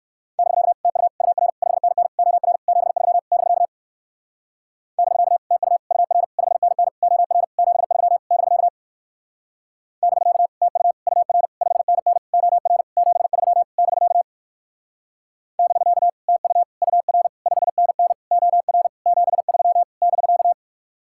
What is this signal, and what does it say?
Morse code, heard as a keyed tone, about 62 wpm, 700 Hz. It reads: <BK> TU UR 5NN OR 73 <BK> <BK> TU UR 5NN OR 73 <BK> <BK> TU UR 5NN OR 73 <BK> <BK> TU UR 5NN OR 73 <BK>